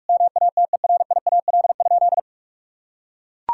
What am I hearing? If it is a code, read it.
Morse code, heard as a keyed tone, about 42 wpm, 700 Hz. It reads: MATERIAL?
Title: Sentences from Top 1000 Words - Review Part 5 - 45wpm